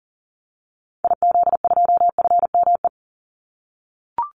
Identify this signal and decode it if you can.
Morse code, heard as a keyed tone, about 40 wpm, 700 Hz. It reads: IZ2FME
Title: Call Signs - 40wpm